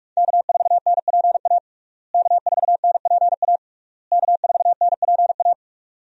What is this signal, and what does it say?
Morse code, heard as a keyed tone, about 46 wpm, 700 Hz. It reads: K4NPA K4NPA K4NPA
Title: USA Call Signs - 5 Characters - Repeated 3x - 45wpm